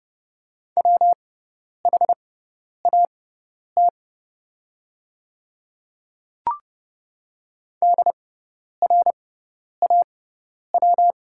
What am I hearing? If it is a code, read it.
Morse code, heard as a keyed tone, about 30 wpm, spaced as for 10 wpm, 690 Hz. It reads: WHAT DRAW
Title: 6x Character Spacing - Top 500 Words - 30wpm